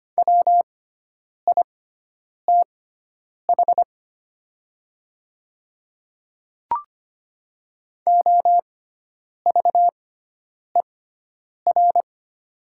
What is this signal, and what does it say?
Morse code, heard as a keyed tone, about 25 wpm, spaced as for 9 wpm, 700 Hz. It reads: WITH OVER
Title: 6x Character Spacing - Top 100 Words - 25wpm